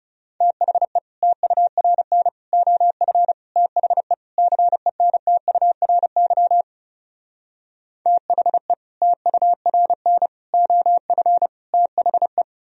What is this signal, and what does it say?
Morse code, heard as a keyed tone, about 38 wpm, 700 Hz. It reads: THE TURN OF THE CENTURY THE TURN OF THE